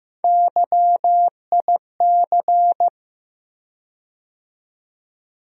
Morse code, heard as a keyed tone, about 15 wpm, 700 Hz. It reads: YIC
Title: USA Call Sign Postfixes - 3 Letters 15wpm